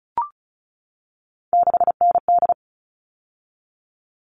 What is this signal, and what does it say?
Morse code, heard as a keyed tone, about 34 wpm, 700 Hz. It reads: 6ND
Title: All Cut Numbers - 3-Digit 35wpm